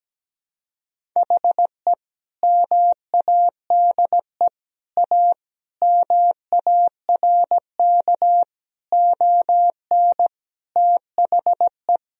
Morse code, heard as a keyed tone, about 17 wpm, 700 Hz. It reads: HE MADE A MARK ON THE
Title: Sentences from Top 1000 Words - Review Part 1 - 17wpm